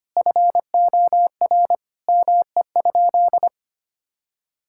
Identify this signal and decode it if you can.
Morse code, heard as a keyed tone, about 25 wpm, 700 Hz. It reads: FOR ME?
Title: Sentences from Top 1600 Words 25wpm